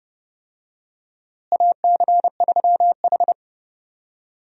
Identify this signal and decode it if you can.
Morse code, heard as a keyed tone, about 30 wpm, 700 Hz. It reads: AC3H
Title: USA Call Signs - 4 Characters - Repeated 2x - 30wpm